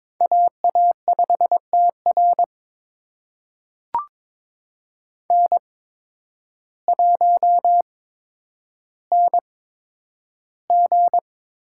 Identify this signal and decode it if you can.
Morse code, heard as a keyed tone, about 22 wpm, spaced as for 10 wpm, 700 Hz. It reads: AA5TR N1NG
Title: USA Call Signs - 8x Character Spacing - 22wpm